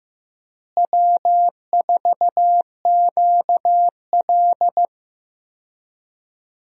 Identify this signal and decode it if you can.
Morse code, heard as a keyed tone, about 15 wpm, 700 Hz. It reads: W4QL
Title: USA Call Signs - 4 Characters - Repeated 3x - 15wpm